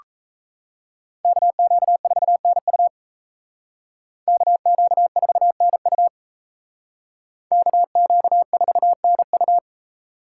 Morse code, heard as a keyed tone, about 34 wpm, 700 Hz. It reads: KQ4NU KQ4NU KQ4NU
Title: Speed Racing - USA Call Signs 28wpm